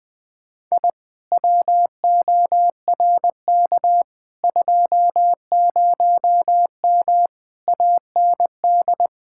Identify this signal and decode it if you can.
Morse code, heard as a keyed tone, about 20 wpm, 700 Hz. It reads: I WORK 20M AND
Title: Speed-Racing Common QSO Phrases Encore - 20wpm